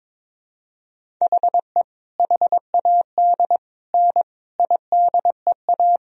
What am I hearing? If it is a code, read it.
Morse code, heard as a keyed tone, about 22 wpm, 700 Hz. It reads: HE HAD N IDEA
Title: Sentences from Top 300 Words and Abbreviations 22wpm